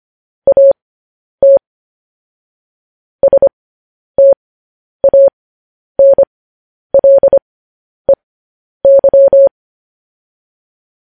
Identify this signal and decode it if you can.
Morse code, heard as a keyed tone, about 25 wpm, spaced as for 10 wpm, 550 Hz. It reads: AT STANLEY
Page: news headlines in morse code at 10 wpm